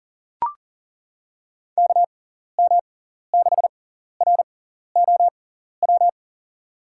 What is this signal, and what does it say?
Morse code, heard as a keyed tone, about 39 wpm, spaced as for 14 wpm, 700 Hz. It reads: KM6ROW
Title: USA Call Signs - 6 Characters - 6x Character Spacing - 40wpm